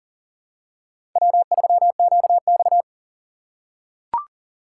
Morse code, heard as a keyed tone, about 40 wpm, 700 Hz. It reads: W3QX
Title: USA Call Signs - 4 Characters - 4x Character Spacing - 40wpm